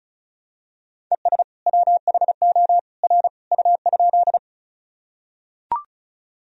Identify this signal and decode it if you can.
Morse code, heard as a keyed tone, about 34 wpm, 700 Hz. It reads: ES WHO R U?
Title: Sentences from Top 100 Words and Abbreviations 35wpm